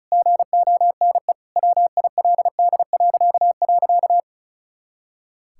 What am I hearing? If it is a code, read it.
Morse code, heard as a keyed tone, about 34 wpm, 700 Hz. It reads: GONE WILD..
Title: Pangram Challenge v2 - 35wpm